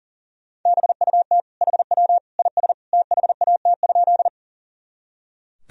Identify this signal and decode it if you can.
Morse code, heard as a keyed tone, about 40 wpm, 700 Hz. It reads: BUT HW IS THAT?
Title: Sentences from Top 100 Words and Abbreviations 40wpm